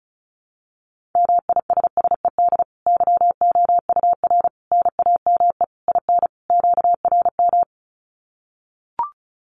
Morse code, heard as a keyed tone, about 34 wpm, 700 Hz. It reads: MISSED YOUR NAME IN QRM
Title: Common QSO Phrases 35wpm